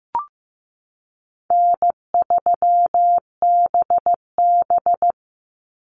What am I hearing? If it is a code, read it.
Morse code, heard as a keyed tone, about 15 wpm, 700 Hz. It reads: N3BB
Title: Call Signs Repeated 3x - 15wpm